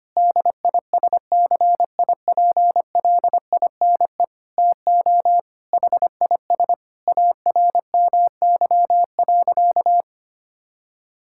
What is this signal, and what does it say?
Morse code, heard as a keyed tone, about 25 wpm, 700 Hz. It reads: DISCIPLINE TO HIS ARMY.
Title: Sentences from Top 1000 Words - Review Part 6 - 25wpm